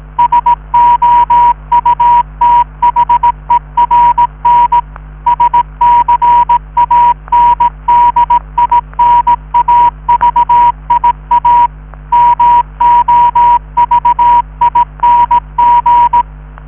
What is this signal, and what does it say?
Morse code, heard as a keyed tone, about 18 wpm, 960 Hz. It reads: SOUTHERN SCANDINAVIA MOVING